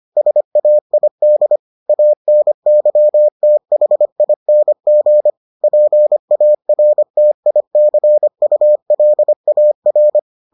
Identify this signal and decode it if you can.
Morse code, heard as a keyed tone, about 25 wpm, 600 Hz. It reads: SAID ANYTHING PARTICULAR